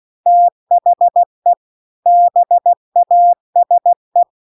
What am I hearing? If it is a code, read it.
Morse code, heard as a keyed tone, about 16 wpm, 700 Hz. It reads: THE BASE
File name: CW-The-Big-Twoheaded-River-2-16WPM